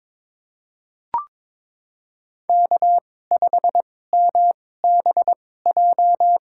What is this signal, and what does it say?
Morse code, heard as a keyed tone, about 22 wpm, spaced as for 16 wpm, 700 Hz. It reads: K5MBJ
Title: USA Call Signs - 5 Characters - 2x Character Spacing - 22wpm